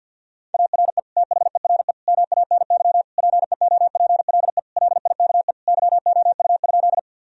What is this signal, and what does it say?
Morse code, heard as a keyed tone, about 50 wpm, 700 Hz. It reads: ARE THERE MANY PEOPLE LIKE YOU?
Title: Sentences from Top 1000 Words - Review Part 1 - 50wpm